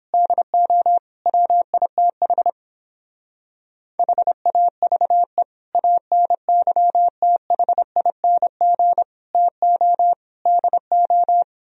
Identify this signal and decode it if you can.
Morse code, heard as a keyed tone, about 29 wpm, 700 Hz. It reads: DO WITH HAVE ANYTHING TO DO